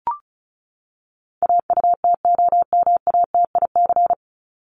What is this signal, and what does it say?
Morse code, heard as a keyed tone, about 34 wpm, 700 Hz. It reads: AUTOMATIC
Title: Top 1751-2000 Words 35wpm